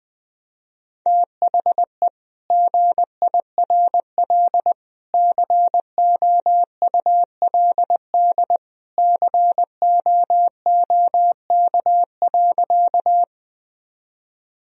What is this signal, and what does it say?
Morse code, heard as a keyed tone, about 21 wpm, 700 Hz. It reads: THE GIRL COULD COOK.
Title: Sentences from Top 1000 Words - Review Part 4 - 20wpm